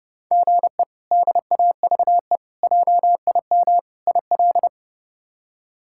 Morse code, heard as a keyed tone, about 30 wpm, 700 Hz. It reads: GE DAVE JIM IL
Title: K1USN SST QSO Practice - 30wpm